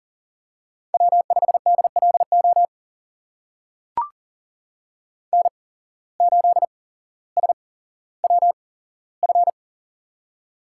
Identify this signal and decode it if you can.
Morse code, heard as a keyed tone, about 42 wpm, spaced as for 17 wpm, 700 Hz. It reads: W5DLO N8SWF
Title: USA Call Signs - 5 Characters - 8x Character Spacing - 40wpm